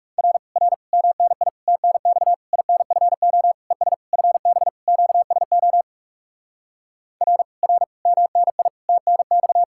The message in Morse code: R R MNI TNX INFO ES FB QSO R R MNI TNX

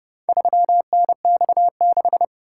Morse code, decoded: 3NX6

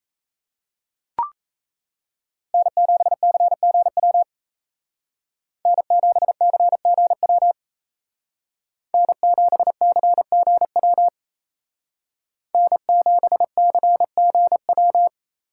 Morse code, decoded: N7CGW N7CGW N7CGW N7CGW

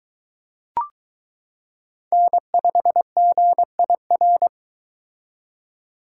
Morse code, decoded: N5GIR

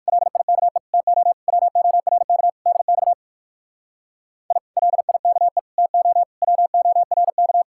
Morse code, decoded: LIKE TO WORK DX I LIKE TO WORK

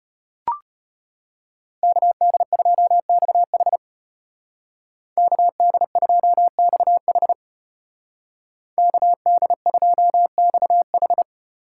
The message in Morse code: KD2XH KD2XH KD2XH

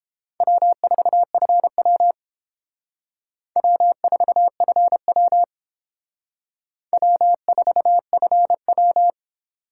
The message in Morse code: W4FW W4FW W4FW

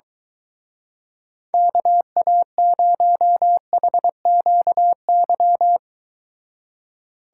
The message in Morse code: KA0HQY

